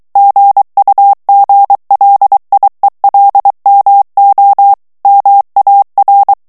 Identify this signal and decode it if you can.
Morse code, heard as a keyed tone, about 23 wpm, 800 Hz. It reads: GUGLIELMO MAR
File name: morse mp3 / ITALIEN texte clair 1 vitesse 22 mots minute